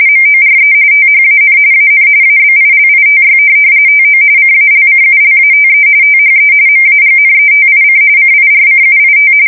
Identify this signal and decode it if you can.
RTTY, about 45.45 baud. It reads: MORE THAN AN ABILITY TO CALCULATE THE MAGNETIC CIRCUIT